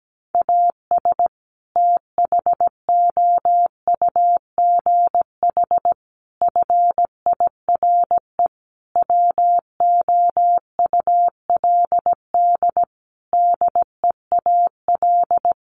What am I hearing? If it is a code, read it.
Morse code, heard as a keyed tone, about 17 wpm, 700 Hz. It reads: AS THOUGH FIRE WOULD DEAL
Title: Sentences from Top 1500 Words 17wpm